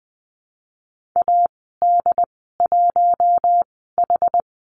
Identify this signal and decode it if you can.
Morse code, heard as a keyed tone, about 20 wpm, spaced as for 14 wpm, 700 Hz. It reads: AD1H
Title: 069 Lesson 12g - Letter d - Call Signs 20wpm